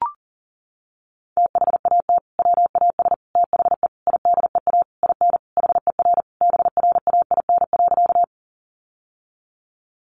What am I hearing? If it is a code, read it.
Morse code, heard as a keyed tone, about 40 wpm, 700 Hz. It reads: THAT WAS THE IDEA IN HER BRAIN.
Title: Sentences from Top 1100 Words 40wpm